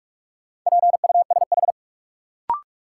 Morse code, heard as a keyed tone, about 41 wpm, 700 Hz. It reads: PUSH
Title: Sentences from Top 1300 Words - New Word Drill Alt - 45wpm